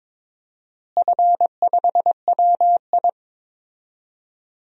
Morse code, heard as a keyed tone, about 22 wpm, 700 Hz. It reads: F5WI
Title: DXCC-100 Call Signs - 4 Characters - Repeated 3x - 22wpm